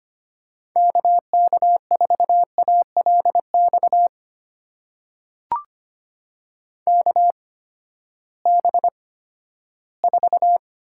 Morse code, heard as a keyed tone, about 25 wpm, spaced as for 14 wpm, 700 Hz. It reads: KK4ALX KB4
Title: USA Call Signs - 6 Characters - 8x Character Spacing - 25wpm